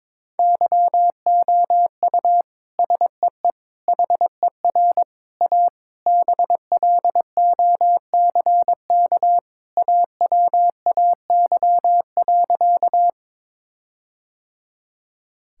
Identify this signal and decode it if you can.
Morse code, heard as a keyed tone, about 22 wpm, 700 Hz. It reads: YOU SEE HER A BLOCK AWAY.
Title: Sentences from Top 1600 Words 22wpm